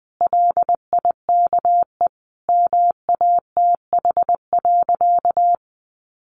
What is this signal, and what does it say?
Morse code, heard as a keyed tone, about 20 wpm, 700 Hz. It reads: LIKE MATH.
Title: Easy Phrases 20wpm